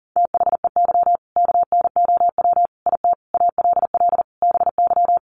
Morse code, heard as a keyed tone, about 40 wpm, 700 Hz. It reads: THEY KNOW IT ALL BY
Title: Sentences from Top 1200 Words 40wpm